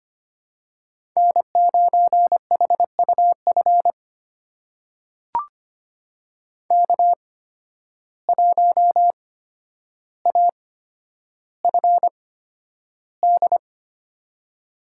Morse code, heard as a keyed tone, about 25 wpm, spaced as for 11 wpm, 700 Hz. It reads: N9HUF K1AFD
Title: USA Call Signs - 5 Characters - 8x Character Spacing - 25wpm